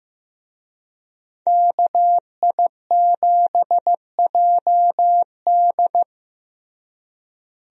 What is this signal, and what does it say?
Morse code, heard as a keyed tone, about 15 wpm, 700 Hz. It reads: KI7JD
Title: USA Call Signs - 5 Characters - Repeated 2x - 15wpm